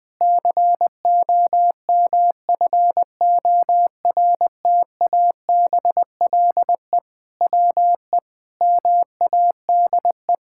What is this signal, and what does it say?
Morse code, heard as a keyed tone, about 20 wpm, 700 Hz. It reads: COMFORTABLE WE MADE
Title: Sentences from Top 1600 Words 20wpm